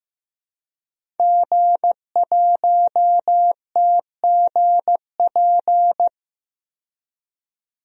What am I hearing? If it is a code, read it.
Morse code, heard as a keyed tone, about 15 wpm, 700 Hz. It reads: G1TGP
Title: DXCC-100 Call Signs - 5 Characters - 15wpm